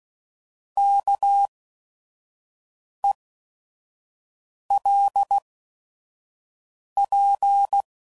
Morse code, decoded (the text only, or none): KELP